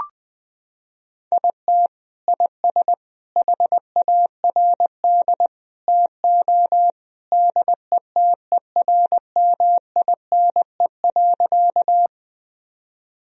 Morse code, heard as a keyed tone, about 20 wpm, 700 Hz. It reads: IT IS HARD TO DETERMINE.
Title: Sentences from Top 1000 Words - Review Part 4 - 20wpm